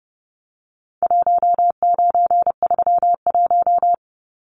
Morse code, heard as a keyed tone, about 30 wpm, 700 Hz. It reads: 1931
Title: Numbers - 4-Digit 30wpm